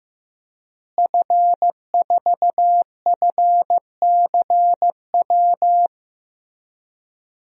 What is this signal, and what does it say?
Morse code, heard as a keyed tone, about 15 wpm, 700 Hz. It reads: F4FCW
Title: DXCC-100 Call Signs - 5 Characters - Repeated 3x - 15wpm